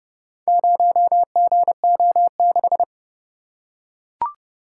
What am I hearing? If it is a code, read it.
Morse code, heard as a keyed tone, about 30 wpm, 700 Hz. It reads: 0GO6